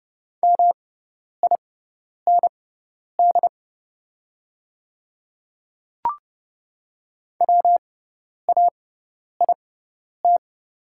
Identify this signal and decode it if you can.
Morse code, heard as a keyed tone, about 30 wpm, spaced as for 10 wpm, 690 Hz. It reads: MIND WAIT
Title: 6x Character Spacing - Top 500 Words - 30wpm